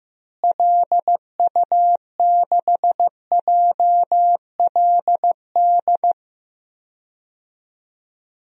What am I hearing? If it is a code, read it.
Morse code, heard as a keyed tone, about 15 wpm, 700 Hz. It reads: LU6JLD
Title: DXCC-100 Call Signs - 6 Characters - Repeated 3x - 15wpm